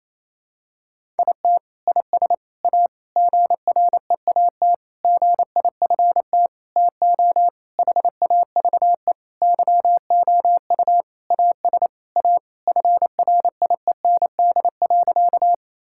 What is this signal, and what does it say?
Morse code, heard as a keyed tone, about 28 wpm, 700 Hz. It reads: IT IS A GREAT GIFT TO HAVE YOU AS A FRIEND.
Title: Sentences from Top 1000 Words - Review Part 6 - 28wpm